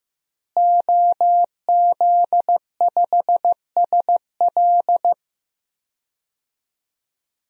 Morse code, heard as a keyed tone, about 15 wpm, 700 Hz. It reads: OZ5SL
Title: DXCC-100 Call Signs - 5 Characters - Repeated 2x - 15wpm